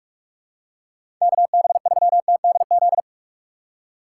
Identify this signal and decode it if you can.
Morse code, heard as a keyed tone, about 46 wpm, 700 Hz. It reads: KB3TDZ
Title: USA Call Signs - 6 Characters - 6x Character Spacing - 45wpm